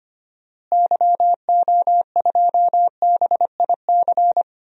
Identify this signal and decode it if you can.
Morse code, heard as a keyed tone, about 25 wpm, 700 Hz. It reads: YO2BIC
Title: DXCC-100 Call Signs - 6 Characters - Repeated 3x - 25wpm